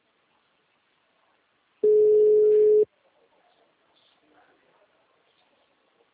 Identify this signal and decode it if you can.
Ringback tone (Europe, 425 Hz).
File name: out- -2005- - - .50